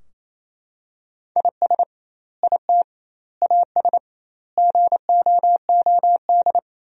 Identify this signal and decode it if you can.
Morse code, heard as a keyed tone, about 28 wpm, 700 Hz. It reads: IS IT AS GOOD
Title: 2x Word-Spacing - Sentences from Top 200 Words - 28wpm